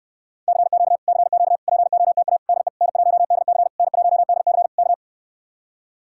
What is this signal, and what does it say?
Morse code, heard as a keyed tone, about 74 wpm, 700 Hz. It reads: CQ CQ POTA DE N1DC N1DC K